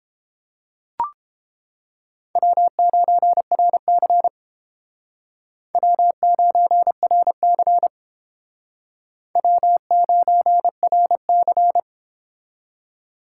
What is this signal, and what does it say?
Morse code, heard as a keyed tone, about 30 wpm, 700 Hz. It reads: W9RC W9RC W9RC